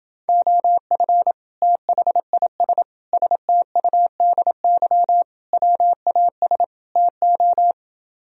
Morse code, heard as a keyed tone, about 29 wpm, 700 Hz. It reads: OF THIS STUDY WAS TO